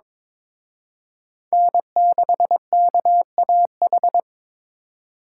Morse code, heard as a keyed tone, about 26 wpm, 700 Hz. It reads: N6KAH
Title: Speed Racing - USA 1x3 Call Signs 22wpm